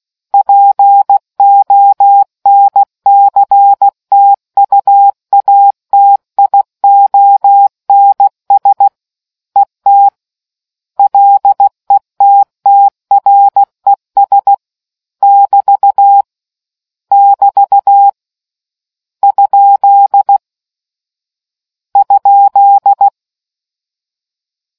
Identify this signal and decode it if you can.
Morse code, heard as a keyed tone, about 16 wpm, 800 Hz. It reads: PONCTUATIONS ET LETTRES <BT> <BT> ? ?